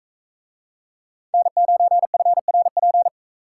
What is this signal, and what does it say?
Morse code, heard as a keyed tone, about 44 wpm, 700 Hz. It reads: N9FRP